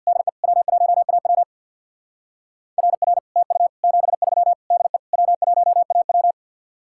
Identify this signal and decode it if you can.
Morse code, heard as a keyed tone, about 61 wpm, 700 Hz. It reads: DE W1AW RR TU 73 DE W1AW
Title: Speed-Racing Common QSO Phrases - 50wpm